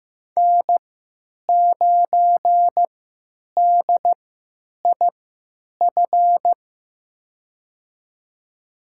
Morse code, heard as a keyed tone, about 15 wpm, spaced as for 9 wpm, 700 Hz. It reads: N9DIF
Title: USA Call Signs - 5 Characters - 3x Character Spacing - 15wpm